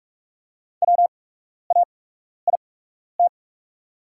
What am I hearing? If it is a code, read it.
Morse code, heard as a keyed tone, about 48 wpm, spaced as for 12 wpm, 690 Hz. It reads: WAIT
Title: 8x Character Spacing - Top 500 Words - 45wpm